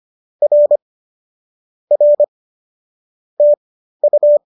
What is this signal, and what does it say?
Morse code, heard as a keyed tone, about 25 wpm, spaced as for 13 wpm, 570 Hz. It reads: R R TU